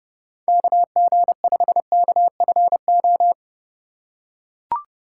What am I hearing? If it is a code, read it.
Morse code, heard as a keyed tone, about 30 wpm, 700 Hz. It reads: KG5KFO
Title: USA Call Signs - 6 Characters - 2x Character Spacing - 30wpm